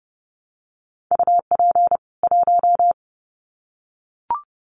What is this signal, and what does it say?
Morse code, heard as a keyed tone, about 30 wpm, 700 Hz. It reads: UP 1
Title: Common QSO Phrases 30wpm